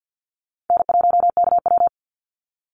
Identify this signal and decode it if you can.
Morse code, heard as a keyed tone, about 50 wpm, 700 Hz. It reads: N1KW
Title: Easy Call Signs - 50wpm